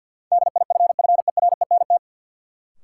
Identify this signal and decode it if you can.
Morse code, heard as a keyed tone, about 53 wpm, 700 Hz. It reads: DIFFERENT
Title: Sentences from Top 600 Words - New Word Drill - 50wpm